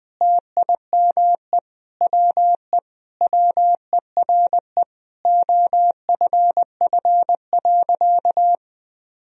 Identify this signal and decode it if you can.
Morse code, heard as a keyed tone, about 20 wpm, 700 Hz. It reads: TIME WE WERE OFF.
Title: Sentences from Top 1000 Words - Review Part 1 - 20wpm